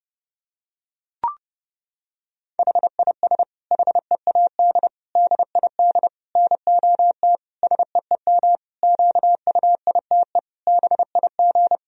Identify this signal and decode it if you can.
Morse code, heard as a keyed tone, about 30 wpm, 700 Hz. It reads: HIS HEAD DID NOT SEEM QUITE BIG
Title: Sentences from Top 1000 Words - Review Part 2 - 30wpm